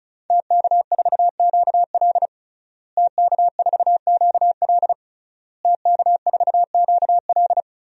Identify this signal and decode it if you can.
Morse code, heard as a keyed tone, about 36 wpm, 700 Hz. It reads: TK4QL TK4QL TK4QL
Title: DXCC-100 Call Signs - 5 Characters - Repeated 3x - 35wpm